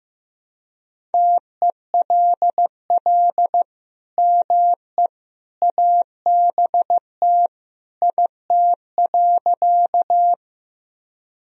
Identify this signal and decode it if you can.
Morse code, heard as a keyed tone, about 15 wpm, 700 Hz. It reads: TELL ME ABT IT.
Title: Sentences from Top 200 Words and Abbreviations 15wpm